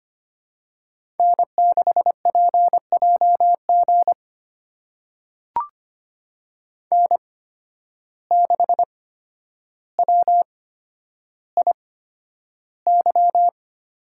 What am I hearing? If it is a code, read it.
Morse code, heard as a keyed tone, about 25 wpm, spaced as for 11 wpm, 700 Hz. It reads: N6PJG N6WIY
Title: USA Call Signs - 5 Characters - 8x Character Spacing - 25wpm